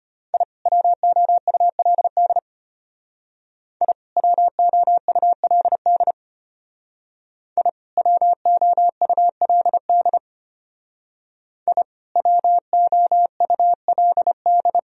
Morse code, heard as a keyed tone, about 30 wpm, 700 Hz. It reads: I WOULD I WOULD I WOULD I WOULD
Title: Speed-Racing - Sets of 2 Words 25wpm